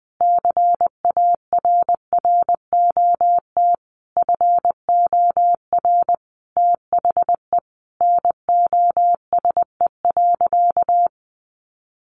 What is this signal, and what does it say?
Morse code, heard as a keyed tone, about 20 wpm, 700 Hz. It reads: CARROT FOR THE NOSE.
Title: Easy Phrases 20wpm